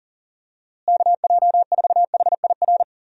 Morse code, heard as a keyed tone, about 37 wpm, 700 Hz. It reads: KJ4HIR
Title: USA Call Signs - 6 Characters - Repeated 3x - 40wpm